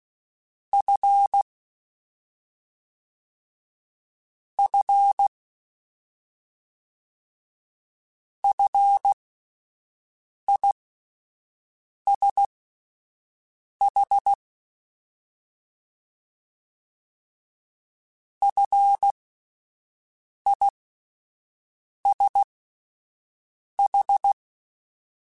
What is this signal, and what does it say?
Morse code, heard as a keyed tone, about 16 wpm, spaced as for 5 wpm, 800 Hz. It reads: F F FISH FISH
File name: Fs / Single F mp3s / Single F Prefixes 3x2 Twice @5